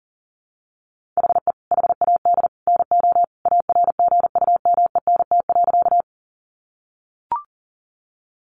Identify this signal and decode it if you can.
Morse code, heard as a keyed tone, about 40 wpm, 700 Hz. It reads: HE HAD NO ARGUMENT.
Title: Sentences from Top 1400 Words 40wpm